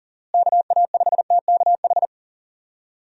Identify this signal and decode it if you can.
Morse code, heard as a keyed tone, about 40 wpm, 700 Hz. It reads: KA5TKH
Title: USA Call Signs - 6 Characters - Repeated 2x - 40wpm